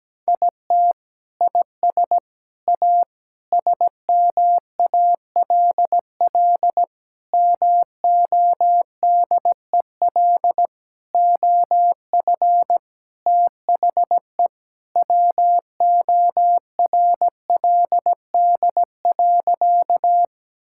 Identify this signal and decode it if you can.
Morse code, heard as a keyed tone, about 17 wpm, 700 Hz. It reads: IT IS A SMALL MODEL OF THE WORLD.
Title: Sentences from Top 1000 Words - Review Part 5 - 17wpm